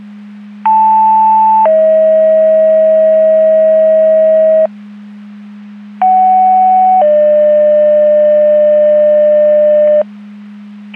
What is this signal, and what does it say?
Two-tone page, heard as a one-second tone, then a three-second tone, about 880 then 640 Hz, about 750 then 600 Hz. It is heard from 0.6 to 10.0 s.
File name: Portage / 02753 20-34 Portage 154.385 Paging